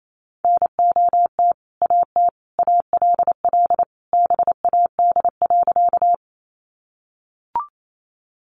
Morse code, heard as a keyed tone, about 28 wpm, 700 Hz. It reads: NOT AT ALL BAD.
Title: Sentences from Top 700 Words 28wpm